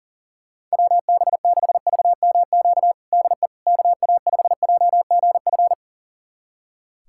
Morse code, heard as a keyed tone, about 40 wpm, 700 Hz. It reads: WB6VMQ DE KA5JGF
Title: USA Callsign DE Callsign - 6 Characters - 40wpm